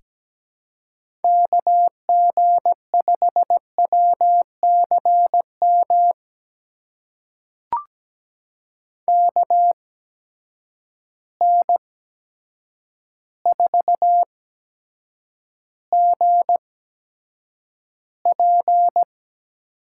Morse code, heard as a keyed tone, about 17 wpm, spaced as for 8 wpm, 700 Hz. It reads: KG5WCM KN4GP
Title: USA Call Signs - 8x Character Spacing - 17wpm